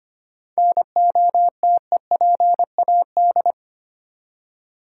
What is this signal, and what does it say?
Morse code, heard as a keyed tone, about 25 wpm, 700 Hz. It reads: NOTEPAD
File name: practice / 25wpm / Easy Words with Hints - 25wpm